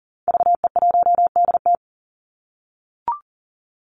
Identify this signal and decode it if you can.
Morse code, heard as a keyed tone, about 40 wpm, 700 Hz. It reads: VE1DT
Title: Call Signs - 40wpm